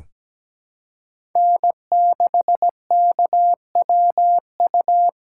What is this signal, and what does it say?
Morse code, heard as a keyed tone, about 17 wpm, 700 Hz. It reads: N6KWU
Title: USA Call Signs - 5 Characters - 3x Character Spacing - 17wpm